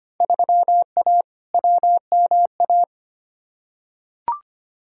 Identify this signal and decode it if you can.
Morse code, heard as a keyed tone, about 25 wpm, 700 Hz. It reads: 3A WMA
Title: Field Day Exchanges Encore - 25wpm